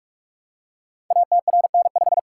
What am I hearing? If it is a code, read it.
Morse code, heard as a keyed tone, about 44 wpm, 700 Hz. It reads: ATRN5
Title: Code Groups - 5 Letter:Numbers - 45wpm